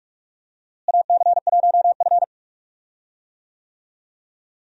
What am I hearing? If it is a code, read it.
Morse code, heard as a keyed tone, about 43 wpm, 700 Hz. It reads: AC1F
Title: USA Call Signs - 4 Characters - Repeated 2x - 45wpm